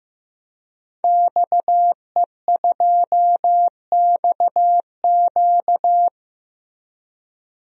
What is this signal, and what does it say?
Morse code, heard as a keyed tone, about 15 wpm, 700 Hz. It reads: XE2XQ
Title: DXCC-100 Call Signs - 5 Characters - 15wpm